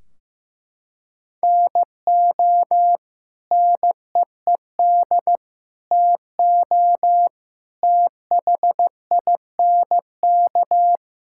Morse code, heard as a keyed tone, about 15 wpm, 700 Hz. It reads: NO NEED TO THINK
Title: Sentences from Top 1600 Words 15wpm